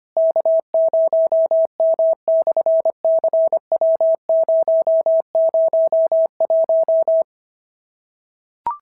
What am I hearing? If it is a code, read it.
Morse code, heard as a keyed tone, about 25 wpm, 650 Hz. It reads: K0M/CW001
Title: SOTA QSO Practice 25wpm